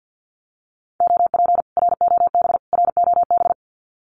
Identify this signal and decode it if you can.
Morse code, heard as a keyed tone, about 48 wpm, 700 Hz. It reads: OP ROB ROB
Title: Common QSO Phrases Encore 50wpm